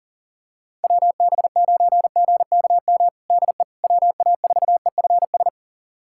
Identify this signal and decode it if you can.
Morse code, heard as a keyed tone, about 41 wpm, 690 Hz. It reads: WB9GKM DE WA4EFS
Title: USA Callsign DE Callsign - 6 Characters - 40wpm